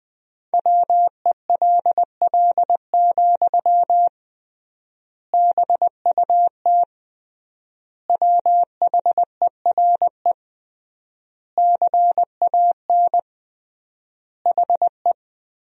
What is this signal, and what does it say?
Morse code, heard as a keyed tone, about 20 wpm, 700 Hz. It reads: WELL, BUT WHERE CAN HE
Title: 3x Word-Spacing - Sentences from Top 500 Words - 20wpm